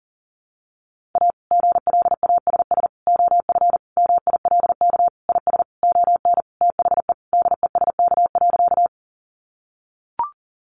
Morse code, heard as a keyed tone, about 40 wpm, 700 Hz. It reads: A GLASS OF MILK IS ON THE DESK.
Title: Additional Easy Phrases 40wpm